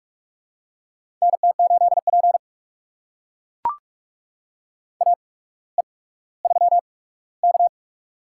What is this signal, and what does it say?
Morse code, heard as a keyed tone, about 43 wpm, spaced as for 19 wpm, 700 Hz. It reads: NT8P AE3K
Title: USA Call Signs - 4 Characters - 8x Character Spacing - 45wpm